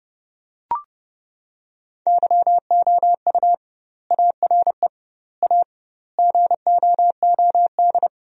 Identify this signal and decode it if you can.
Morse code, heard as a keyed tone, about 30 wpm, 700 Hz. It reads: YOU ARE A GOOD
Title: 2x Word-Spacing - Sentences from Top 200 Words - 30wpm